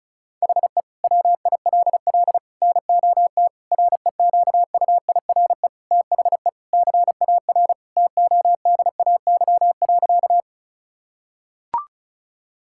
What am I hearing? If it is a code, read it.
Morse code, heard as a keyed tone, about 34 wpm, 700 Hz. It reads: HE WILL NOT REQUIRE THE CAR TODAY.
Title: Sentences from Top 1000 Words - Review Part 6 - 35wpm